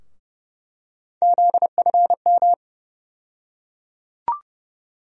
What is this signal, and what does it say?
Morse code, heard as a keyed tone, about 30 wpm, 700 Hz. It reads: ZFM
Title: USA Call Sign Postfixes - 3 Letters 30wpm